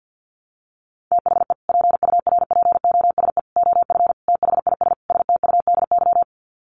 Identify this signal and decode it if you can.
Morse code, heard as a keyed tone, about 50 wpm, 700 Hz. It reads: THE PURPOSE OF THIS STUDY
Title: Sets of 5 Words 50wpm